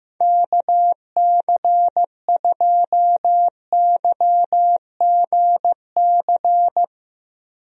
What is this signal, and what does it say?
Morse code, heard as a keyed tone, about 15 wpm, 700 Hz. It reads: KC2YGC
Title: USA Call Signs - 6 Characters - 15wpm